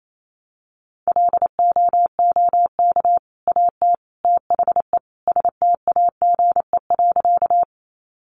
Morse code, heard as a keyed tone, about 28 wpm, 700 Hz. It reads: LOOK AT THE STAGE.
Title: Sentences from Top 1200 Words 28wpm